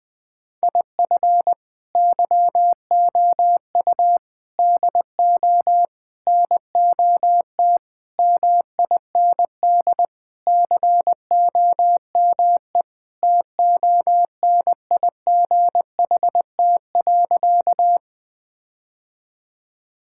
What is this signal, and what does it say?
Morse code, heard as a keyed tone, about 20 wpm, 700 Hz. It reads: IF YOU DO NOT MIND COME TONIGHT.
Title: Sentences with 4-Word Phrases - 20wpm